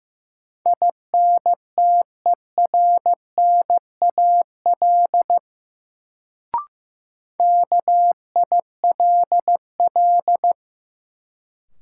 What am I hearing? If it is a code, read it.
Morse code, heard as a keyed tone, about 15 wpm, 700 Hz. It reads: INTERNAL KILL
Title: Sentences from Top 1400 Words - New Word Drill Alt - 15wpm